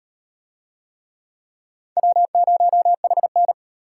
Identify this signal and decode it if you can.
Morse code, heard as a keyed tone, about 37 wpm, 700 Hz. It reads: W0HN